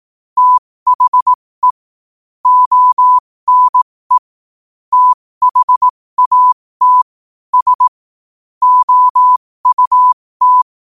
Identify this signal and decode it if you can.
Morse code, heard as a keyed tone, about 18 wpm, spaced as for 15 wpm, 1000 Hz. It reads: THE ONE THAT S OUT
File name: qotdfiles / 15wpm